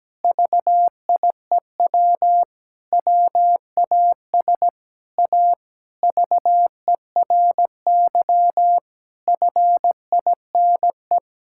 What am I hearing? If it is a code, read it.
Morse code, heard as a keyed tone, about 17 wpm, 700 Hz. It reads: VIEW WAS A VERY FINE
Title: Sentences from Top 1000 Words - Review Part 6 - 17wpm